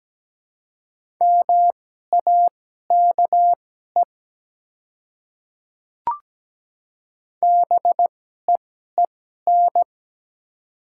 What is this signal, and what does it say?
Morse code, heard as a keyed tone, about 17 wpm, spaced as for 12 wpm, 700 Hz. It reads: MAKE BEEN
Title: 2x Character Spacing - Top 200 Words - 17wpm